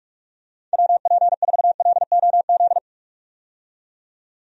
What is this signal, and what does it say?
Morse code, heard as a keyed tone, about 46 wpm, 700 Hz. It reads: WP4LOZ
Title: USA Call Signs - 6 Characters - 45wpm